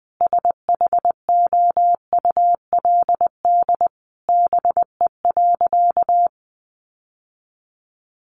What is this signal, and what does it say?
Morse code, heard as a keyed tone, about 20 wpm, 700 Hz. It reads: SHOULD BE.
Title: Sentences from Top 900 Words 20wpm